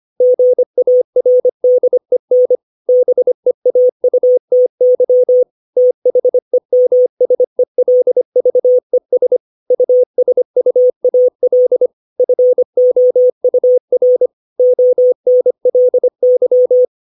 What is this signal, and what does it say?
Morse code, heard as a keyed tone, about 25 wpm, 500 Hz. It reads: GARDEN BEAUTY THEMSELVES USUAL FOUR ONLY